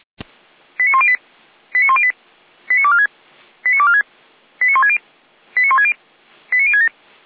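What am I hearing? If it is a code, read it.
Five-tone selective call (ZVEI), heard as quick runs of five tones, tones of about 70 ms. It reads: 89108 89108 89246 89246 89160 89160 89067
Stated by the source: ZVEI five-tone selective call 89078 received by SDR.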